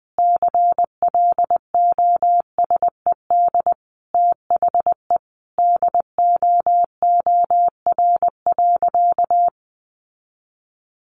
Morse code, heard as a keyed tone, about 20 wpm, 700 Hz. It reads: CLOSED THE DOOR.
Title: Sentences from Top 1400 Words 20wpm